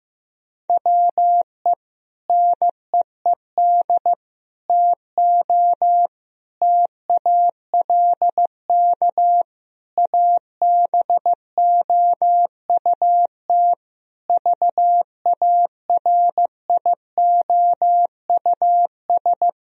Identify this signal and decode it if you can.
Morse code, heard as a keyed tone, about 16 wpm, 700 Hz. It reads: WE NEED TO TALK ABOUT VARIOUS
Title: Sentences from Top 1000 Words - Review Part 4 - 15wpm